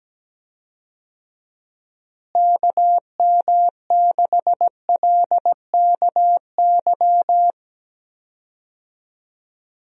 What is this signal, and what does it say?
Morse code, heard as a keyed tone, about 17 wpm, 700 Hz. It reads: KM6LKY
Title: USA Call Signs - 6 Characters - Repeated 2x - 17wpm